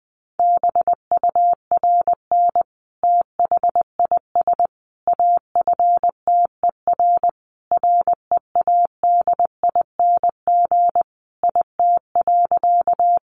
Morse code, heard as a keyed tone, about 20 wpm, 700 Hz. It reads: BURN THIS AFTER READING IT.
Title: Sentences from Top 1300 Words 20wpm